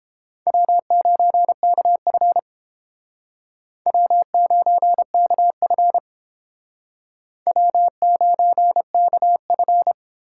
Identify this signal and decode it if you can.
Morse code, heard as a keyed tone, about 31 wpm, 700 Hz. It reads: W9KF W9KF W9KF